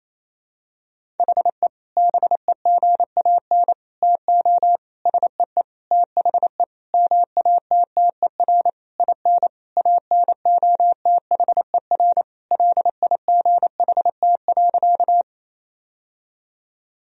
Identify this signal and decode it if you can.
Morse code, heard as a keyed tone, about 28 wpm, 700 Hz. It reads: HE BEGAN TO SEE THE MATTER IN ANOTHER LIGHT.
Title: Sentences from Top 1000 Words - Review Part 1 - 28wpm